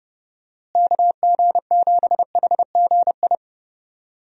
Morse code, heard as a keyed tone, about 30 wpm, 700 Hz. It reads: KG7HGI
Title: USA Call Signs - 6 Characters - Repeated 2x - 30wpm